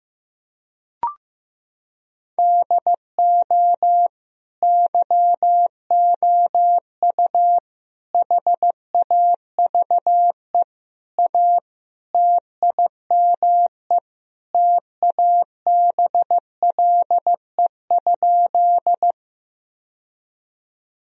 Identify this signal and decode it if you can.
Morse code, heard as a keyed tone, about 16 wpm, 700 Hz. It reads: DO YOU HAVE A TIME TABLE?
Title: Sentences from Top 1000 Words - Review Part 2 - 15wpm